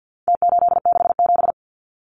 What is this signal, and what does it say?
Morse code, heard as a keyed tone, about 50 wpm, 700 Hz. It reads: T867
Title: Numbers - Contest Style - 4-Digit 50wpm